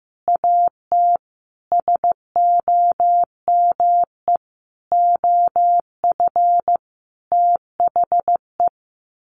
Morse code, heard as a keyed tone, about 15 wpm, 700 Hz. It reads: AT SOME OF THE
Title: Sets of 5 Words Encore 15wpm